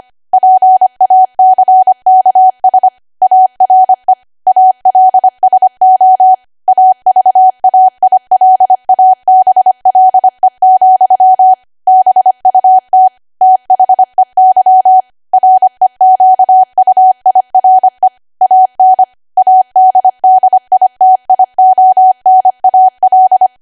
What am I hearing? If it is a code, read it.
Morse code, heard as a keyed tone, about 25 wpm, 750 Hz. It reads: PACKS ARE ALSO AVAILABLE, BUT THEY REQUIRE AN ADDITIONAL
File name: Archive / 25 WPM / 25WPM